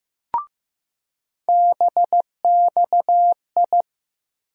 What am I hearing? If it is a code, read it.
Morse code, heard as a keyed tone, about 15 wpm, 700 Hz. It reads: BXI
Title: USA Call Sign Postfixes - 3 Letters 15wpm